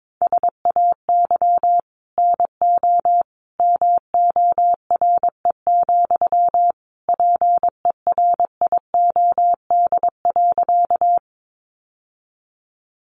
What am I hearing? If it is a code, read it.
Morse code, heard as a keyed tone, about 22 wpm, 700 Hz. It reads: SAY NO MORE, PERIOD.
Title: Sentences from Top 800 Words 22wpm